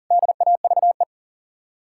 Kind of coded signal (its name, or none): Morse code